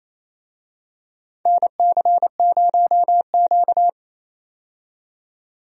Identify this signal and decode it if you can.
Morse code, heard as a keyed tone, about 28 wpm, 700 Hz. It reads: NC0Q
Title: USA Call Signs - 4 Characters - 28wpm